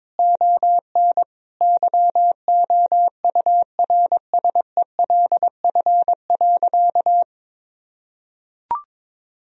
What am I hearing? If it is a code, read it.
Morse code, heard as a keyed tone, about 22 wpm, 700 Hz. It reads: ON YOURSELF.
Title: Sentences from Top 1000 Words - Review Part 2 - 22wpm